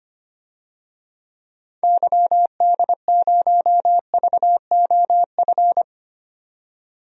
Morse code, heard as a keyed tone, about 25 wpm, 700 Hz. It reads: YD0VOF
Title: DXCC-100 Call Signs - 6 Characters - 25wpm